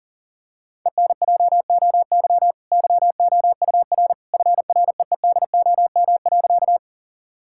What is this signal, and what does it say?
Morse code, heard as a keyed tone, about 40 wpm, 700 Hz. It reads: ENJOY YOUR FREEDOM.
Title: Sentences from Top 1000 Words - Review Part 5 - 40wpm